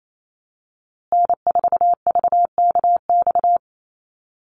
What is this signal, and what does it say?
Morse code, heard as a keyed tone, about 28 wpm, 700 Hz. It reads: N4VKX
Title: USA Call Signs Repeated 3x 28wpm